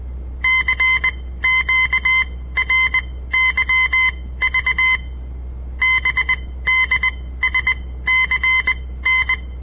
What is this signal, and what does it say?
Morse code, heard as a keyed tone, about 20 wpm, spaced as for 15 wpm, 2050 Hz. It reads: CQRYV BDSCN
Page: On the air Morse code practice from the Southern Montana Amatuer Radio Association; all code, no read back